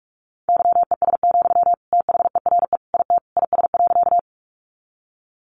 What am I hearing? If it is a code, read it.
Morse code, heard as a keyed tone, about 44 wpm, 700 Hz. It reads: YES, THERE IT IS.
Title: Sentences from Top 500 Words 45wpm